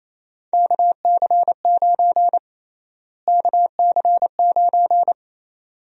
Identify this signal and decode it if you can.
Morse code, heard as a keyed tone, about 27 wpm, 700 Hz. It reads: KC9 KC9
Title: USA Call Sign Prefixes - Repeated 2x - 28wpm